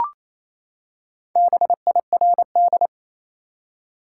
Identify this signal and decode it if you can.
Morse code, heard as a keyed tone, about 28 wpm, 700 Hz. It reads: BIRD
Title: Four Letter Words - 28wpm